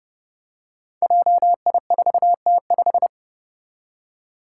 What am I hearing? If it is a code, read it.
Morse code, heard as a keyed tone, about 30 wpm, 700 Hz. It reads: JI4T5
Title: Five Challenging Letter-Number Combinations 30wpm